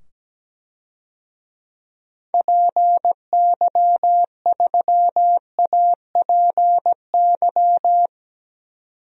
Morse code, heard as a keyed tone, about 17 wpm, 700 Hz. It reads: PY3APY
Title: DXCC-100 Call Signs - 6 Characters - 17wpm